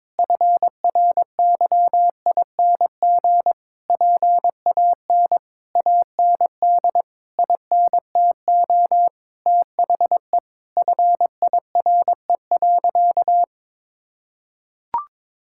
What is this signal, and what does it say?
Morse code, heard as a keyed tone, about 22 wpm, 700 Hz. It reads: FRYING PAN AND INTO THE FIRE.
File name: Sentences with 9-Word Phrases - 22wpm